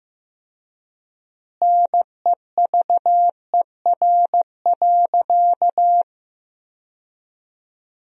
Morse code, heard as a keyed tone, about 15 wpm, 700 Hz. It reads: NEVER.
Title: 4x Word-Spacing - Sentences from Top 300 Words - 15wpm